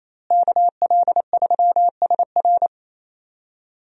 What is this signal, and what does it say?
Morse code, heard as a keyed tone, about 28 wpm, 700 Hz. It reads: KL3SR
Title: USA Call Signs - 5 Characters - Repeated 3x - 28wpm